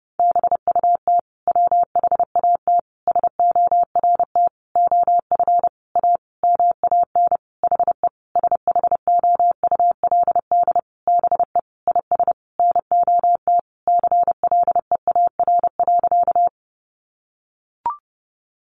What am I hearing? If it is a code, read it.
Morse code, heard as a keyed tone, about 30 wpm, 700 Hz. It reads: BUT WHAT SORT OF A MAN HE SHOULD BE IS NOT CLEAR.
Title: Sentences from Top 1300 Words 30wpm